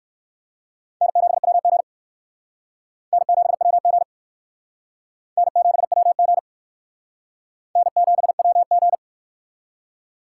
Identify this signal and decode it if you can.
Morse code, heard as a keyed tone, about 57 wpm, 700 Hz. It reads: N7WG N7WG N7WG N7WG